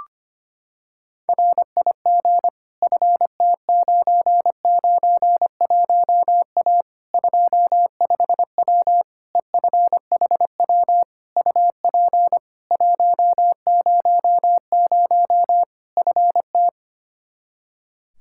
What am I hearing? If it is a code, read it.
Morse code, heard as a keyed tone, about 25 wpm, 700 Hz. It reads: RIG FT991A 25W EFHW UP 100 FT
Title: Practice QSO - Exchange 2 - Info - 25wpm